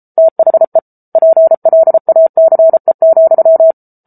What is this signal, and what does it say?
Morse code, heard as a keyed tone, about 34 wpm, 650 Hz. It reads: THE PLACE,